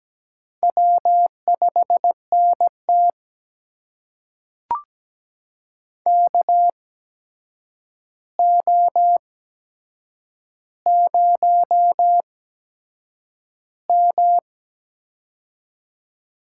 Morse code, heard as a keyed tone, about 17 wpm, spaced as for 7 wpm, 700 Hz. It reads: W5NT KO0M
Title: USA Call Signs - 4 Characters - 8x Character Spacing - 17wpm